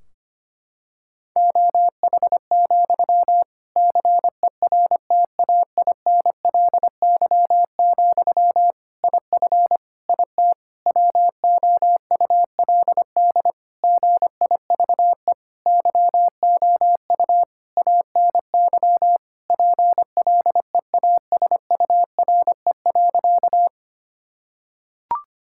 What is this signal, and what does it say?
Morse code, heard as a keyed tone, about 25 wpm, 700 Hz. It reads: OH, CERTAINLY, IF IT WOULD GIVE YOU ANY PLEASURE.
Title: Sentences from Top 1600 Words 25wpm